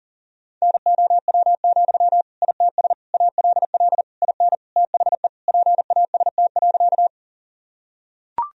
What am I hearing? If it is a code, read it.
Morse code, heard as a keyed tone, about 40 wpm, 700 Hz. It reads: NOW, ITS ALL IN THE PAST.
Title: Sentences from Top 1000 Words - Review Part 3 - 40wpm